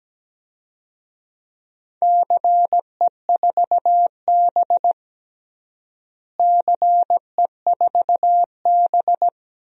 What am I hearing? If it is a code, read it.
Morse code, heard as a keyed tone, about 17 wpm, 700 Hz. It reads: CE4B CE4B
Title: DXCC-100 Call Signs - 4 Characters - Repeated 2x - 17wpm